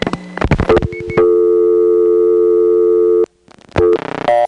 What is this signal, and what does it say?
Dial tone (North America), 350 + 440 Hz.
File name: GroupBell / MariettaMap-unfinished / Marietta1